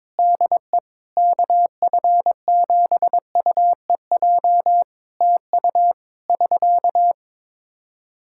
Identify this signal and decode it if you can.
Morse code, heard as a keyed tone, about 22 wpm, 700 Hz. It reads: DE KF7UEJ TU <SK>
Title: K1USN SST QSO Practice 22wpm